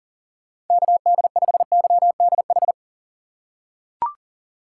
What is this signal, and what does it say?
Morse code, heard as a keyed tone, about 40 wpm, 700 Hz. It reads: KD5YDH
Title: USA Call Signs - 6 Characters - 2x Character Spacing - 40wpm